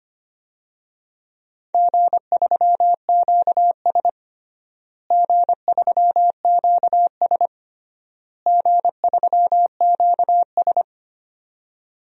Morse code, heard as a keyed tone, about 25 wpm, 700 Hz. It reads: G3QS G3QS G3QS
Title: DXCC-100 Call Signs - 4 Characters - Repeated 3x - 25wpm